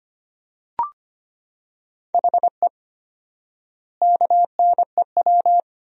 Morse code, heard as a keyed tone, about 25 wpm, 700 Hz. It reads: HE KNEW
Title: 4x Word-Spacing - Sentences from Top 400 Words - 25wpm